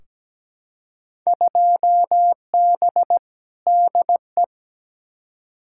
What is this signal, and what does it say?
Morse code, heard as a keyed tone, about 17 wpm, 700 Hz. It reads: 2B DE
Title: Field Day Exchanges Encore - 17wpm